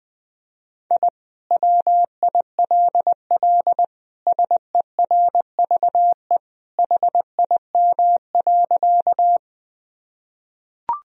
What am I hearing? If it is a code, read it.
Morse code, heard as a keyed tone, about 21 wpm, 700 Hz. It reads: I WILL SERVE HIM.
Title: Sentences from Top 1000 Words - Review Part 2 - 20wpm